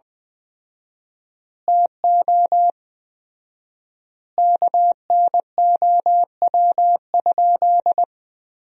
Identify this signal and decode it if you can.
Morse code, heard as a keyed tone, about 21 wpm, 700 Hz. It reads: TO KNOW?
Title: 4x Word-Spacing - Sentences from Top 200 Words - 20wpm